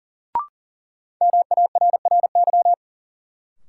Morse code, heard as a keyed tone, about 40 wpm, 700 Hz. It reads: MARRY
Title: Sentences from Top 1200 Words - New Word Drill - 40wpm